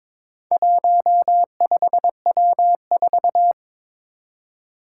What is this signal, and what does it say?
Morse code, heard as a keyed tone, about 22 wpm, 700 Hz. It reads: 15W4
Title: Four Challenging Letter-Number Combinations 22wpm